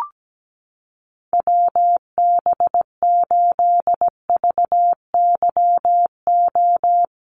Morse code, heard as a keyed tone, about 17 wpm, 700 Hz. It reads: WB8VYO
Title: USA Call Signs Repeated 3x 17wpm